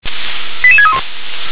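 Five-tone selective call (ZVEI), heard as quick runs of five tones, tones of about 70 ms. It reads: 88631
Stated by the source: Received by SDR; ZVEI five-tone call.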